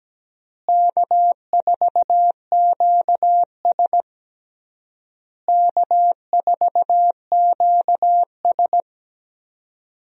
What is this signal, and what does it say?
Morse code, heard as a keyed tone, about 17 wpm, 700 Hz. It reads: K4QS K4QS
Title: USA Call Signs - 4 Characters - Repeated 3x - 17wpm